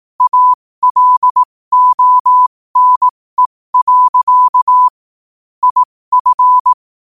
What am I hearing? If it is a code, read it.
Morse code, heard as a keyed tone, about 18 wpm, spaced as for 15 wpm, 1000 Hz. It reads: ALONE. IF
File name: qotdfiles / 15wpm